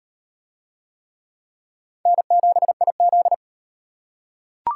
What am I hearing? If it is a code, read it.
Morse code, heard as a keyed tone, about 38 wpm, 700 Hz. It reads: N7IZ